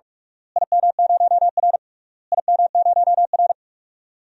Morse code, heard as a keyed tone, about 46 wpm, 690 Hz. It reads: IM0R IM0R
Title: DXCC-100 Call Signs - 4 Characters - Repeated 3x - 45wpm